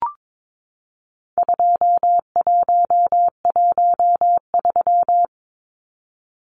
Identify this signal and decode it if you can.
Morse code, heard as a keyed tone, about 22 wpm, 700 Hz. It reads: 2113
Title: Numbers - 4-Digit 22wpm